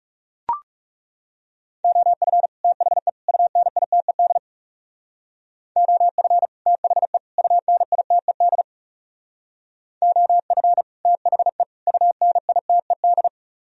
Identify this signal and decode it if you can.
Morse code, heard as a keyed tone, about 37 wpm, 700 Hz. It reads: OF THE UNITED OF THE UNITED OF THE UNITED
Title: Speed-Racing - Sets of 3 Words 30wpm